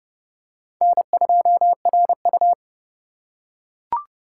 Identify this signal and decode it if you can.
Morse code, heard as a keyed tone, about 27 wpm, 700 Hz. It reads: N2RU